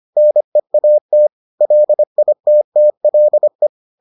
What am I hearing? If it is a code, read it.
Morse code, heard as a keyed tone, about 25 wpm, 600 Hz. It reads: NEAT LITTLE